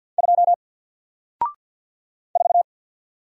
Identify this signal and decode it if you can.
Morse code, heard as a keyed tone, about 50 wpm, 700 Hz. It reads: 2 4
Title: Single Letter-Number 50wpm